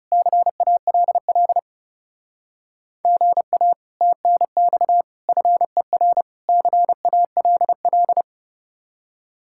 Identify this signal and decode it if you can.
Morse code, heard as a keyed tone, about 39 wpm, 700 Hz. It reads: CALL GA TNX FER CALL